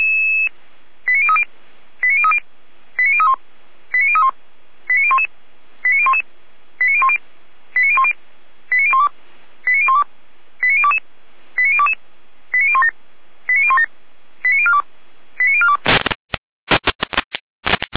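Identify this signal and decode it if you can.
Five-tone selective call (ZVEI), heard as quick runs of five tones, tones of about 70 ms. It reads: 89030 89030 89031 89031 89011 89011 89010 89010 89012 89012 89022 89022 89017 89017 89042 89042
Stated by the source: ZVEI five-tone call received on SDR.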